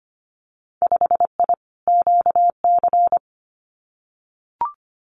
Morse code, heard as a keyed tone, about 25 wpm, 700 Hz. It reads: 5I QC
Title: Winter Field Day Exchanges Encore 25wpm